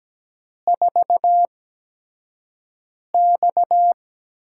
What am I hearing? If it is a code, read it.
Morse code, heard as a keyed tone, about 17 wpm, spaced as for 5 wpm, 700 Hz. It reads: 4X
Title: USA Call Signs - 4 Characters - 8x Character Spacing - 17wpm